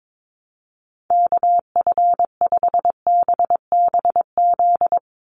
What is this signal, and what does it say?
Morse code, heard as a keyed tone, about 22 wpm, 700 Hz. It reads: KF5BBZ
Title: USA Call Signs 22wpm